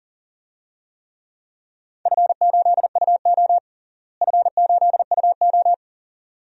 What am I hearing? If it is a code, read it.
Morse code, heard as a keyed tone, about 40 wpm, 700 Hz. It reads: F8UO F8UO
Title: DXCC-100 Call Signs - 4 Characters - Repeated 2x - 40wpm